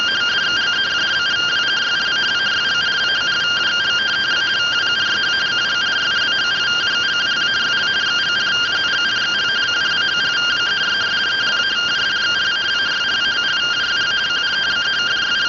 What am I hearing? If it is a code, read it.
Two-tone teleprinter signal, about 170 Hz shift.